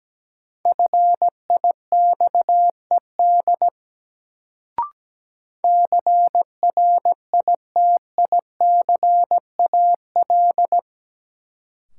Morse code, heard as a keyed tone, about 17 wpm, 700 Hz. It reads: FIXED CRITICAL
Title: Sentences from Top 1100 Words - New Word Drill Alt - 17wpm